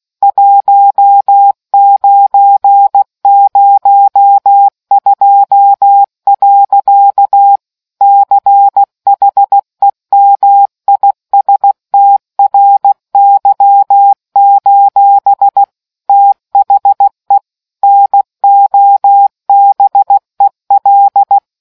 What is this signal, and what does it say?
Morse code, heard as a keyed tone, about 16 wpm, 800 Hz. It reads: 1902. CHEMISTRY: THE NOBEL